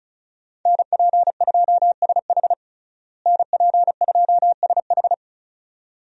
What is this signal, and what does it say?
Morse code, heard as a keyed tone, about 34 wpm, 700 Hz. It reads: NP2SH NP2SH
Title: USA Call Signs - 5 Characters - Repeated 2x - 35wpm